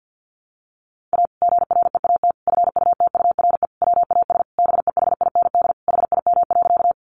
Morse code, heard as a keyed tone, about 50 wpm, 700 Hz. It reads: A GREAT FUTURE WAS BEHIND HIM.
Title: Sentences from Top 700 Words 50wpm